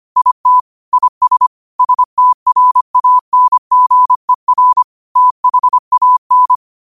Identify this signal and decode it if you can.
Morse code, heard as a keyed tone, about 25 wpm, 1000 Hz. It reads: IT IS STRANGER THAN
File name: qotdfiles / 25wpm